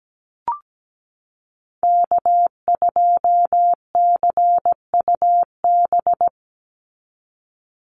Morse code, heard as a keyed tone, about 17 wpm, 700 Hz. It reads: K2CUB
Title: Call Signs - 17wpm